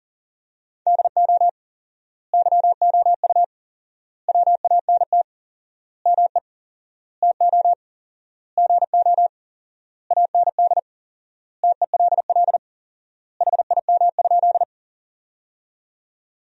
Morse code, heard as a keyed tone, about 40 wpm, 700 Hz. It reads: DO YOU WANT ME TO GO AND TELL HIM?
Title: 4x Word-Spacing - Sentences from Top 200 Words - 40wpm